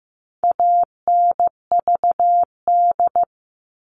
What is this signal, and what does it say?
Morse code, heard as a keyed tone, about 15 wpm, 700 Hz. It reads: ANVD
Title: All Cut Numbers - 4-Digit 15wpm